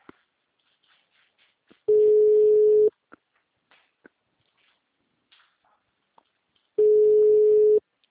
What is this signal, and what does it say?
Ringback tone (Europe, 425 Hz).